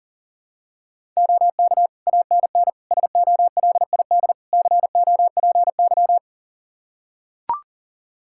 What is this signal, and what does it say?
Morse code, heard as a keyed tone, about 40 wpm, 700 Hz. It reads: OK ANN SOLID COPY
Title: K1USN SST QSO Practice 40wpm